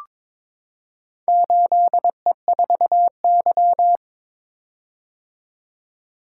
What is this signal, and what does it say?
Morse code, heard as a keyed tone, about 23 wpm, 700 Hz. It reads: 8E4Y
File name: practice / 22wpm / Four Letter-Number Combinations - 22wpm